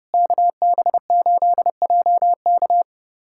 Morse code, heard as a keyed tone, about 30 wpm, 700 Hz. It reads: KB8JK
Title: USA Callsign DE Callsign - 5 Characters - 30wpm